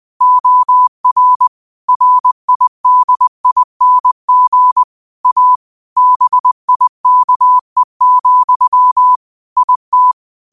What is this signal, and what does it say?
Morse code, heard as a keyed tone, about 20 wpm, 1000 Hz. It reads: OR RIDING A BIKE, IT